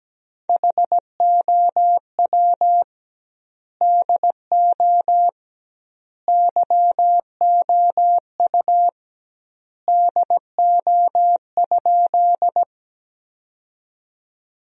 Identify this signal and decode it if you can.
Morse code, heard as a keyed tone, about 17 wpm, 700 Hz. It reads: HOW DO YOU DO?
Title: 2x Word-Spacing - Sentences from Top 100 Words - 17wpm